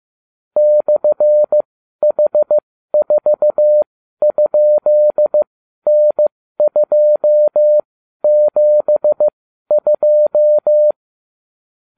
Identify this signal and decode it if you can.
Morse code, heard as a keyed tone, about 15 wpm, spaced as for 12 wpm, 600 Hz. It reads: /H4?N272